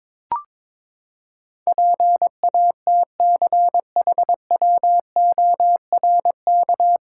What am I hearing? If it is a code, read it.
Morse code, heard as a keyed tone, about 22 wpm, 700 Hz. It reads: PATCHWORK
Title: Pangram Challenge v2 - 22wpm